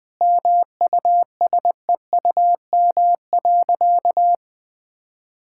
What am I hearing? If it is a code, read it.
Morse code, heard as a keyed tone, about 20 wpm, 700 Hz. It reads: MUSEUM.
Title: Easy Sentences with Hints - 20wpm